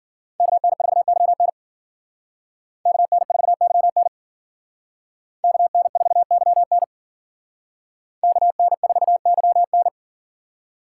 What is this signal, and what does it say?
Morse code, heard as a keyed tone, about 48 wpm, 700 Hz. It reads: KN4YN KN4YN KN4YN KN4YN
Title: Speed Racing - USA Call Signs 40wpm